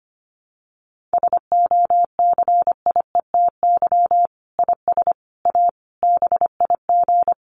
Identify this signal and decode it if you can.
Morse code, heard as a keyed tone, about 25 wpm, 700 Hz. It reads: SOCIETY IS A BIG
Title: Sentences from Top 1500 Words 25wpm